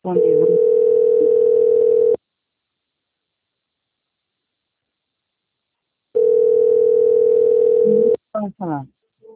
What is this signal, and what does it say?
Ringback tone (North America), 440 + 480 Hz, with speech.
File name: out- - - - - .47872